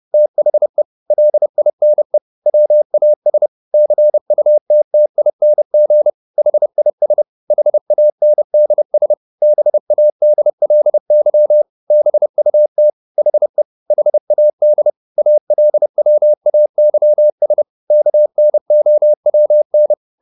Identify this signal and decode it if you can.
Morse code, heard as a keyed tone, about 30 wpm, 600 Hz. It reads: THE LINE WAS CUTTING HIS HANDS BADLY BUT HE HAD ALWAYS KNOWN